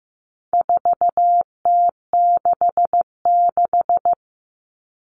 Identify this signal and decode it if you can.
Morse code, heard as a keyed tone, about 15 wpm, 700 Hz. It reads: 4T66
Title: All Cut Numbers - 4-Digit 15wpm